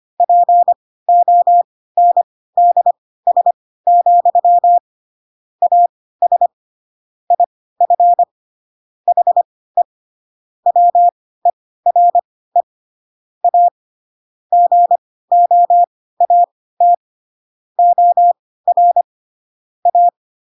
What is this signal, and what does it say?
Morse code, heard as a keyed tone, about 25 wpm, spaced as for 16 wpm, 700 Hz. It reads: PONDS, AS IF HE WERE A GOAT OR A
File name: Pinocchio / Pinocchio ch4 10 25 ik1hge